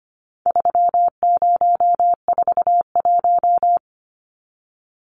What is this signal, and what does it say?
Morse code, heard as a keyed tone, about 25 wpm, 700 Hz. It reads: 3041
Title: Numbers - 4-Digit 25wpm